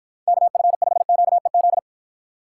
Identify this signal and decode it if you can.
Morse code, heard as a keyed tone, about 52 wpm, 700 Hz. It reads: KF5QEZ